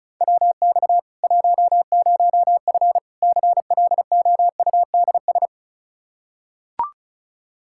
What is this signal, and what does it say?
Morse code, heard as a keyed tone, about 36 wpm, 700 Hz. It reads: WX 10F CLOUDS
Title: Practice QSO - Exchange 2 - Info - 35wpm